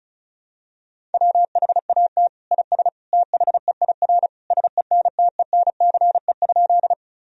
Morse code, heard as a keyed tone, about 34 wpm, 700 Hz. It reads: WHAT IS THEIR SENTENCE?
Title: Sentences from Top 1000 Words - Review Part 1 - 35wpm